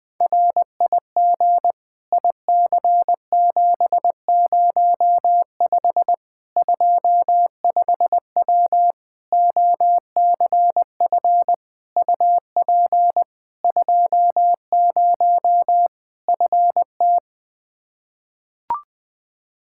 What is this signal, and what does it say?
Morse code, heard as a keyed tone, about 20 wpm, 700 Hz. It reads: RIG IC705 25W OCF UP 20 FT
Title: K1USN SST QSO Practice 20wpm